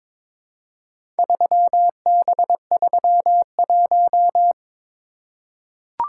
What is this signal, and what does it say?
Morse code, heard as a keyed tone, about 22 wpm, 700 Hz. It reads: 3B31
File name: practice / 22wpm / Four Letter-Number Combinations - 22wpm